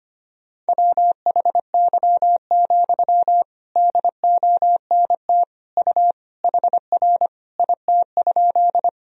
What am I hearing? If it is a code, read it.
Morse code, heard as a keyed tone, about 25 wpm, 700 Hz. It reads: WHY, DONT U HR IT?
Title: Sentences from Top 300 Words and Abbreviations 25wpm